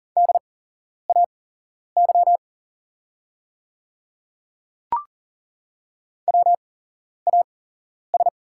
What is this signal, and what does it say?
Morse code, heard as a keyed tone, about 40 wpm, spaced as for 11 wpm, 690 Hz. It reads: DAY WAS
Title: 8x Character Spacing - Top 100 Words - 40wpm